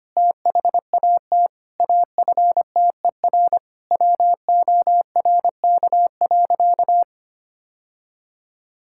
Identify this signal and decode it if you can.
Morse code, heard as a keyed tone, about 25 wpm, 700 Hz. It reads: THAT AFTER WORK.
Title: Sentences from Top 1000 Words - Review Part 5 - 25wpm